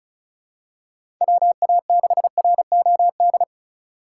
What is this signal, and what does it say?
Morse code, heard as a keyed tone, about 36 wpm, 700 Hz. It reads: WA6ROD
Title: USA Call Signs - 6 Characters - 2x Character Spacing - 35wpm